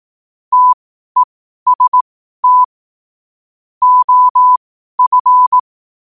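Morse code, heard as a keyed tone, about 18 wpm, spaced as for 12 wpm, 1000 Hz. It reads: TEST OF
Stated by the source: Morse code quote-of-the-day practice at 12 words per minute.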